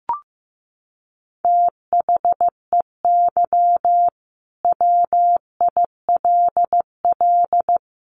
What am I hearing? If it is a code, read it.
Morse code, heard as a keyed tone, about 14 wpm, 700 Hz. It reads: THEY WILL
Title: Sets of 5 Words Encore 15wpm